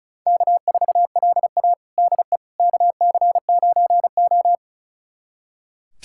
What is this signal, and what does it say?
Morse code, heard as a keyed tone, about 36 wpm, 700 Hz. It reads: K4LA DE KC9O
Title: USA Callsign DE Callsign - 4 Characters - 35wpm